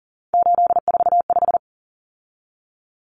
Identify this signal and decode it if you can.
Morse code, heard as a keyed tone, about 40 wpm, 700 Hz. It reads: 845
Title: Common Cut Numbers - 3-Digit 40wpm